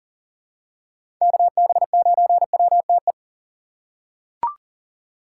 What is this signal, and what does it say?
Morse code, heard as a keyed tone, about 40 wpm, 700 Hz. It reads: KB9WTE
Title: USA Call Signs - 8x Character Spacing - 40wpm